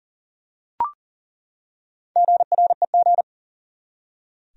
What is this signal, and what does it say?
Morse code, heard as a keyed tone, about 40 wpm, 700 Hz. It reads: GREG
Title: USA Common Names - 40wpm